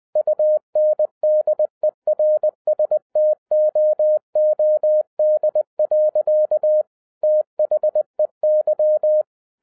Morse code, heard as a keyed tone, about 20 wpm, 610 Hz. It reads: UNDERSTOOD. THEY